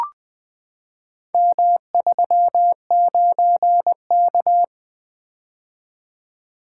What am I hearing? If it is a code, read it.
Morse code, heard as a keyed tone, about 20 wpm, 700 Hz. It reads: M39K
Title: Four Letter-Number Combinations - 20wpm